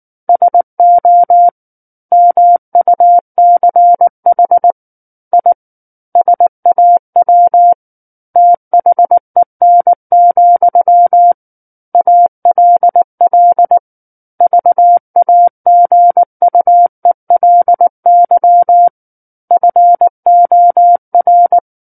Morse code, heard as a keyed tone, about 19 wpm, 700 Hz. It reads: SO MUCH I SAW THEN, ALL VAGUELY FOR
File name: Book One / 10-Chapter 10 - 19 WPM